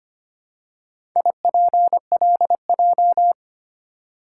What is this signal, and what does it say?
Morse code, heard as a keyed tone, about 25 wpm, 700 Hz. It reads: IPLJ
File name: practice / 25wpm / Four Letter-Number Combinations - 25wpm